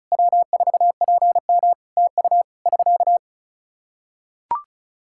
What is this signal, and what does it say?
Morse code, heard as a keyed tone, about 36 wpm, 700 Hz. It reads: W4PM TU <SK>
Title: Practice QSO - Exchange 4 - Ending - 35wpm